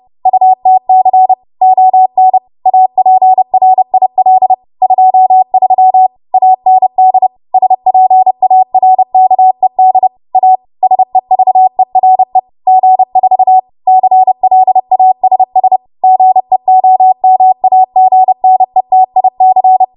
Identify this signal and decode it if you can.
Morse code, heard as a keyed tone, about 30 wpm, 750 Hz. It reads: UTC ON APRIL 23 AND SPARKED A SEVERE G4 CLASS GEOMAGNETIC
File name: Morse / 30WPM